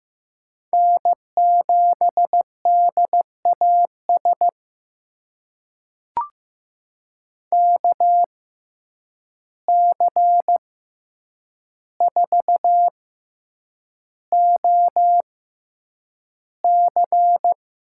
Morse code, heard as a keyed tone, about 15 wpm, spaced as for 8 wpm, 700 Hz. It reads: N7DAS KC4OC
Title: USA Call Signs - 6x Character Spacing - 15wpm